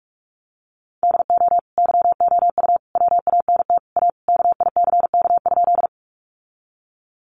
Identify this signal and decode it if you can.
Morse code, heard as a keyed tone, about 45 wpm, 700 Hz. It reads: DO YOU WANT A KICK?
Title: Sentences from Top 1300 Words 45wpm